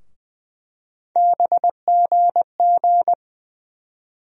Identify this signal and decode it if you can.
Morse code, heard as a keyed tone, about 20 wpm, 700 Hz. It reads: BGG
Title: USA Call Sign Postfixes - 3 Letters 20wpm